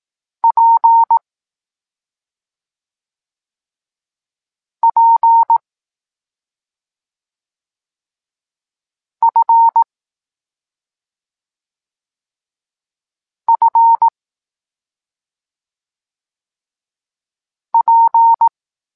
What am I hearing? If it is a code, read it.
Morse code, heard as a keyed tone, about 18 wpm, 920 Hz. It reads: P P F F P
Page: Tekst 25 – Nauka liter P F